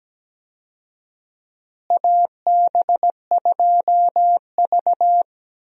Morse code, heard as a keyed tone, about 17 wpm, 700 Hz. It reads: AB2V
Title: USA Call Signs - 4 Characters - Repeated 2x - 17wpm